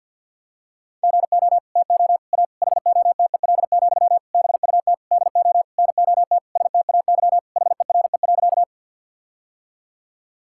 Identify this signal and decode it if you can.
Morse code, heard as a keyed tone, about 50 wpm, 700 Hz. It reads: GO TO A HOTEL, BUT DO NOT STAY HERE.
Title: Sentences from Top 1600 Words 50wpm